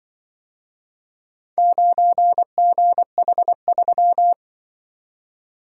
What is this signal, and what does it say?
Morse code, heard as a keyed tone, about 25 wpm, 700 Hz. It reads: 9GH3
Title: Speed-Racing - Four Letter-Number Combinations - 20wpm